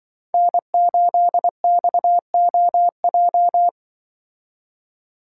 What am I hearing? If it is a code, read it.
Morse code, heard as a keyed tone, about 25 wpm, 700 Hz. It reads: N8XOJ
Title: Speed Racing - USA 1x3 Call Signs 20wpm